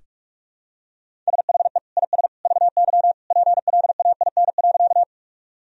Morse code, heard as a keyed tone, about 48 wpm, 700 Hz. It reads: SHE IS VY PLAIN.
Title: Sentences from Top 400 Words and Abbreviations 45wpm